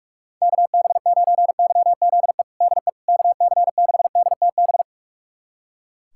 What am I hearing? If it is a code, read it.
Morse code, heard as a keyed tone, about 44 wpm, 700 Hz. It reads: KB9YZE DE KC6DTB
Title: USA Callsign DE Callsign - 6 Characters - 45wpm